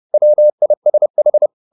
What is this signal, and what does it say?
Morse code, heard as a keyed tone, about 30 wpm, 600 Hz. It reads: WISH